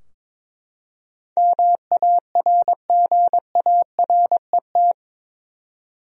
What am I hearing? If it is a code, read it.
Morse code, heard as a keyed tone, about 22 wpm, 700 Hz. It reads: MARGARET
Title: USA Common Names - 22wpm